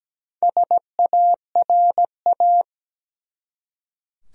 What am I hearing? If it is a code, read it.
Morse code, heard as a keyed tone, about 17 wpm, 700 Hz. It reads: SARA
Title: USA Common Names - 17wpm